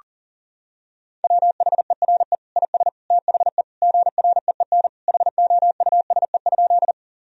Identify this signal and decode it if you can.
Morse code, heard as a keyed tone, about 40 wpm, 700 Hz. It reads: WHERE IS THE GREEN HOUSE?
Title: Sentences from Top 1000 Words - Review Part 2 - 40wpm